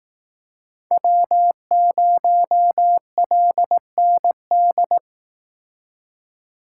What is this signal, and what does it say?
Morse code, heard as a keyed tone, about 19 wpm, 700 Hz. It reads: W0LND